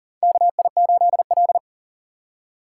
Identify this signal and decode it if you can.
Morse code, heard as a keyed tone, about 40 wpm, 700 Hz. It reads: KI8L
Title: USA Call Signs - 4 Characters - 4x Character Spacing - 40wpm